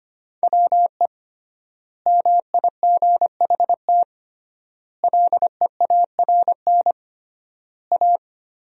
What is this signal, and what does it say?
Morse code, heard as a keyed tone, about 25 wpm, 700 Hz. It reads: WE MIGHT LEARN A
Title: 3x Word-Spacing - Sentences from Top 300 Words - 25wpm